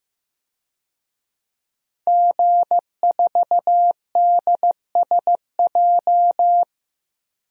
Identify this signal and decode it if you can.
Morse code, heard as a keyed tone, about 15 wpm, 700 Hz. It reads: G4DSJ
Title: DXCC-100 Call Signs - 5 Characters - 15wpm